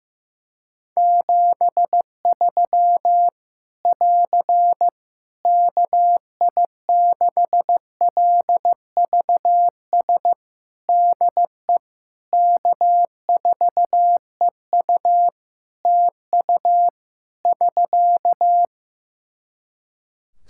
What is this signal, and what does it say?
Morse code, heard as a keyed tone, about 15 wpm, 700 Hz. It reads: 73 <AR> KI6LVS DE K4EU TU <SK>
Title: K1USN SST QSO Practice 15wpm